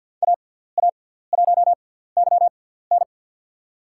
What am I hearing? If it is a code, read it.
Morse code, heard as a keyed tone, about 47 wpm, 700 Hz. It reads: A A 1 Y N
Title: USA Call Signs - 5 Characters - 6x Character Spacing - 50wpm